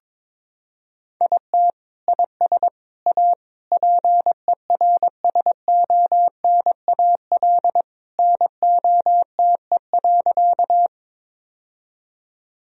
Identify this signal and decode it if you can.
Morse code, heard as a keyed tone, about 22 wpm, 700 Hz. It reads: IT IS A PERSONAL NOTE.
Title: Sentences from Top 1000 Words - Review Part 4 - 22wpm